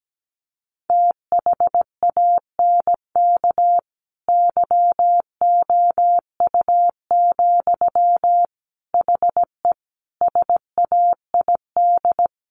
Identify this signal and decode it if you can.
Morse code, heard as a keyed tone, about 17 wpm, 700 Hz. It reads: THANK YOU, HE SAID
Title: Sentences from Top 1500 Words 17wpm